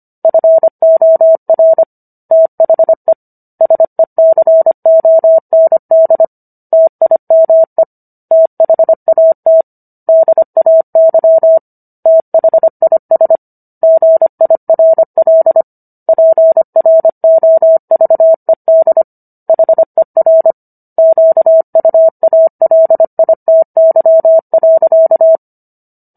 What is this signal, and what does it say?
Morse code, heard as a keyed tone, about 25 wpm, 650 Hz. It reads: FOR THE SECOND TIME THAT DAY THIS GIRL PROVED HER QUALITY.